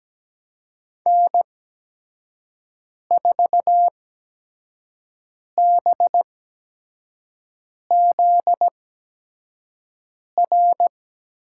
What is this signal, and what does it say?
Morse code, heard as a keyed tone, about 17 wpm, spaced as for 5 wpm, 700 Hz. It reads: N4BZR
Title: USA Call Signs - 5 Characters - 8x Character Spacing - 17wpm